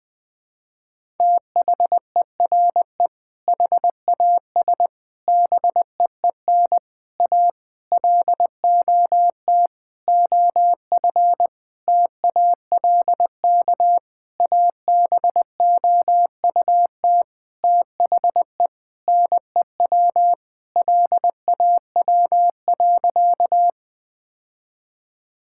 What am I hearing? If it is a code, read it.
Morse code, heard as a keyed tone, about 21 wpm, 700 Hz. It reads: THERE HAS BEEN A LOT OF TALK ABOUT THE NEW LAW.
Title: Sentences with 8-Word Phrases - Spoken Beginning - 20wpm